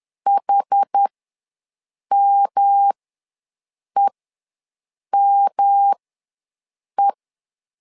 Morse code, heard as a keyed tone, about 10 wpm, spaced as for 6 wpm, 800 Hz. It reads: HMEME
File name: LEZIONI / 3 MWRO / 3 M ws 10wpm 2121